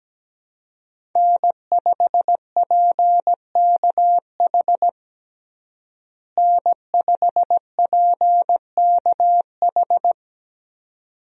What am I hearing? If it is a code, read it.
Morse code, heard as a keyed tone, about 17 wpm, 700 Hz. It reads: N5PKH N5PKH
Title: USA Call Signs - 5 Characters - Repeated 2x - 17wpm